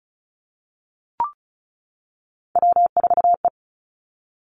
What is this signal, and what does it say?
Morse code, heard as a keyed tone, about 36 wpm, 700 Hz. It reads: W4E
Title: Three Challenging Letter-Number Combinations 35wpm